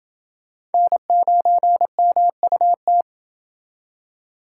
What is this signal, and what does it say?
Morse code, heard as a keyed tone, about 29 wpm, 700 Hz. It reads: N9MUT